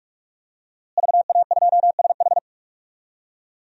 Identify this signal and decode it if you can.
Morse code, heard as a keyed tone, about 46 wpm, 700 Hz. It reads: VA2SH
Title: DXCC-100 Call Signs - 5 Characters - Repeated 3x - 45wpm